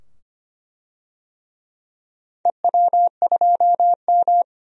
Morse code, heard as a keyed tone, about 25 wpm, 700 Hz. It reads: EW2M
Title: DXCC-100 Call Signs - 4 Characters - 25wpm